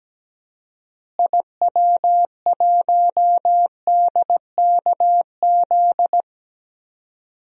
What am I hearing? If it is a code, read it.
Morse code, heard as a keyed tone, about 17 wpm, 700 Hz. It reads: IW1DKZ
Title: DXCC-100 Call Signs - 6 Characters - Repeated 3x - 17wpm